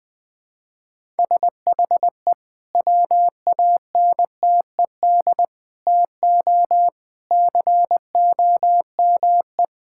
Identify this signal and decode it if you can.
Morse code, heard as a keyed tone, about 20 wpm, 700 Hz. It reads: SHE WANTED TO COME
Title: Sentences with 2-Word Phrases - 20wpm